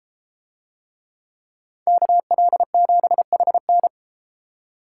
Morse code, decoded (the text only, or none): KL7HN